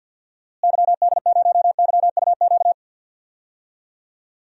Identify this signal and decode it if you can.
Morse code, heard as a keyed tone, about 53 wpm, 700 Hz. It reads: YD0YUQ
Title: DXCC-100 Call Signs - 6 Characters - Repeated 2x - 50wpm